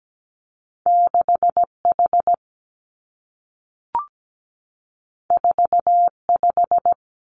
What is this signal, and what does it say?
Morse code, heard as a keyed tone, about 17 wpm, 700 Hz. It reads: 6 H 4 5
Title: Two Challenging Letter-Number Combinations 17wpm